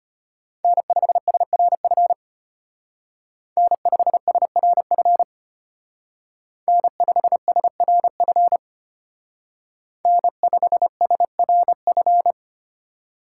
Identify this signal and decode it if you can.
Morse code, heard as a keyed tone, about 31 wpm, 700 Hz. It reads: N5SRF N5SRF N5SRF N5SRF